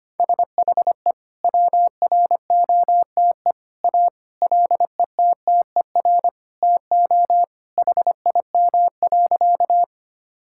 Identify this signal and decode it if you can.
Morse code, heard as a keyed tone, about 25 wpm, 700 Hz. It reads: SHE WROTE A LETTER TO HIM.
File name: practice / 25wpm / Sentences with 2-Word Phrases - 25wpm